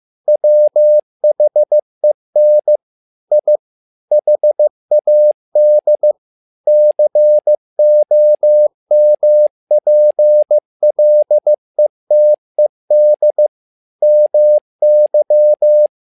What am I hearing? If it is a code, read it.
Morse code, heard as a keyed tone, about 15 wpm, 600 Hz. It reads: WHEN I HAD COMPLETED MY
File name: learning center / tower trouble / tower trouble 15wpm